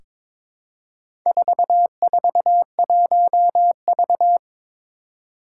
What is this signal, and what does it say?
Morse code, heard as a keyed tone, about 22 wpm, 700 Hz. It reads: 441V
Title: Four Challenging Letter-Number Combinations 22wpm